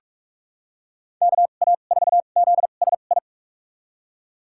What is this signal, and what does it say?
Morse code, heard as a keyed tone, about 46 wpm, spaced as for 32 wpm, 700 Hz. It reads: KA4ZSI
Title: USA Call Signs - 2x Character Spacing - 45wpm